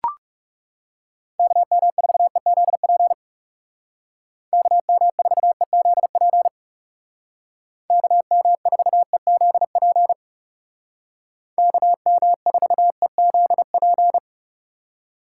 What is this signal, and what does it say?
Morse code, heard as a keyed tone, about 36 wpm, 700 Hz. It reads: KM4EZP KM4EZP KM4EZP KM4EZP
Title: Speed Racing - USA 2x3 Call Signs 30wpm